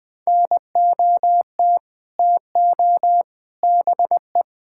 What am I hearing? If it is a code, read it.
Morse code, heard as a keyed tone, about 21 wpm, 700 Hz. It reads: NOT TO BE
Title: Sets of 6 Words - 20wpm